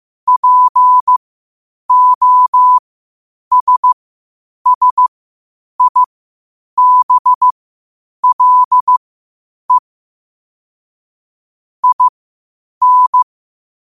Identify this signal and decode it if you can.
Morse code, heard as a keyed tone, about 15 wpm, spaced as for 8 wpm, 1000 Hz. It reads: POSSIBLE IN
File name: qotdfiles / 8wpm